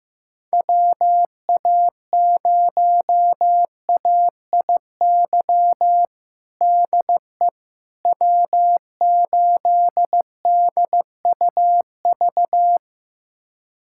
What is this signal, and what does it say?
Morse code, heard as a keyed tone, about 15 wpm, 700 Hz. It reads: WA0AIY DE W8DUV
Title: USA Callsign DE Callsign 15wpm